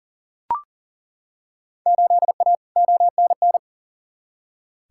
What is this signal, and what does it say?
Morse code, heard as a keyed tone, about 40 wpm, 700 Hz. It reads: 8A ONN
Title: Field Day Exchanges Encore - 40wpm